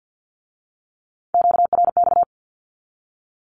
Morse code, heard as a keyed tone, about 50 wpm, 700 Hz. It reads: QRX
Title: Q-Codes with Spoken Meaning - 50wpm